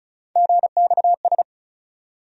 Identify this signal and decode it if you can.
Morse code, heard as a keyed tone, about 34 wpm, 700 Hz. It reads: GXS
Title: USA Call Sign Postfixes - 3 Letters 35wpm